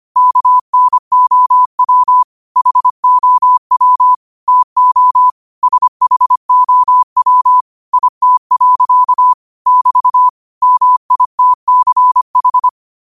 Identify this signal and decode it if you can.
Morse code, heard as a keyed tone, about 25 wpm, 1000 Hz. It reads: KNOW HOW TO SHOW IT. <BT> MITCH